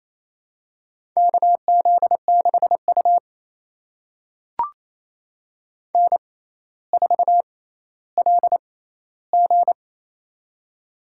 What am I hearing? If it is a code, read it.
Morse code, heard as a keyed tone, about 28 wpm, spaced as for 14 wpm, 700 Hz. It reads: KZ6U N4LG
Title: USA Call Signs - 4 Characters - 6x Character Spacing - 28wpm